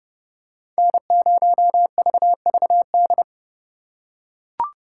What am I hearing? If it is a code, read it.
Morse code, heard as a keyed tone, about 31 wpm, 700 Hz. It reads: N0VVD